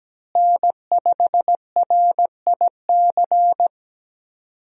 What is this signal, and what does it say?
Morse code, heard as a keyed tone, about 17 wpm, 700 Hz. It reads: N5RIC
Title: USA Call Signs - 5 Characters - Repeated 3x - 17wpm